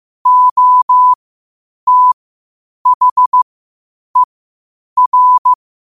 Morse code, heard as a keyed tone, about 15 wpm, spaced as for 8 wpm, 1000 Hz. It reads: OTHER
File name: qotdfiles / 8wpm